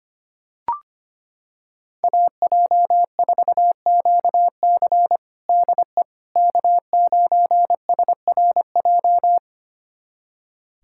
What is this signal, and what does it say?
Morse code, heard as a keyed tone, about 25 wpm, 700 Hz. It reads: AJ4QC DE K9SRJ
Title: USA Callsign DE Callsign - 5 Characters - 25wpm